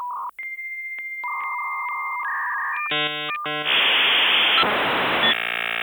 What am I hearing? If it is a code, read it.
Dial-up modem handshake.